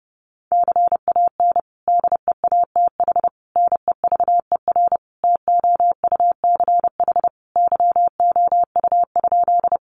Morse code, heard as a keyed tone, about 30 wpm, 700 Hz. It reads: CAN DEATH NEVER TOUCH YOU?
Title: Sentences from Top 1200 Words 30wpm